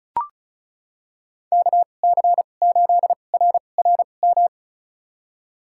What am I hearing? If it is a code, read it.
Morse code, heard as a keyed tone, about 34 wpm, spaced as for 25 wpm, 700 Hz. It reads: KC8RRM
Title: USA Call Signs - 6 Characters - 2x Character Spacing - 35wpm